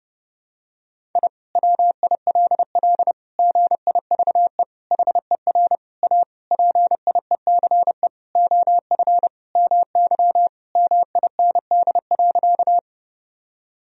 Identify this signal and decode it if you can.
Morse code, heard as a keyed tone, about 30 wpm, 700 Hz. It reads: I WILL GIVE HER A PIECE OF MY MIND.
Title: Sentences from Top 1000 Words - Review Part 2 - 30wpm